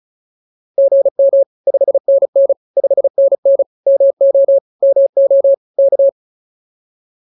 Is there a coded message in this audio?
Yes: Morse code from 0.8 to 6.1 s, heard as a keyed tone, about 34 wpm, 550 Hz. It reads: GM 5NN 5NN MO MO K